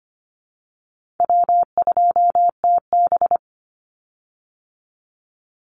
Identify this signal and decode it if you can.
Morse code, heard as a keyed tone, about 25 wpm, 700 Hz. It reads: W2TB
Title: Easy Call Signs Repeated 2x - 25wpm